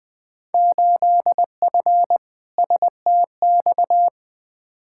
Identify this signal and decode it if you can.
Morse code, heard as a keyed tone, about 20 wpm, 700 Hz. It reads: 8F STX
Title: Field Day Exchanges Encore - 20wpm